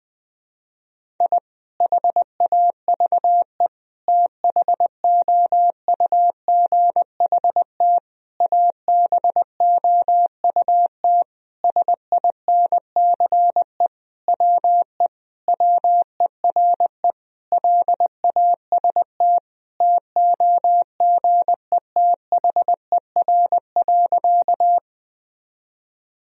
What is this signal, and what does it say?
Morse code, heard as a keyed tone, about 20 wpm, 700 Hz. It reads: I HAVE THOUGHT ABOUT SINCE WE WERE LAST TOGETHER.
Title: Sentences from Top 1000 Words - Review Part 1 - 20wpm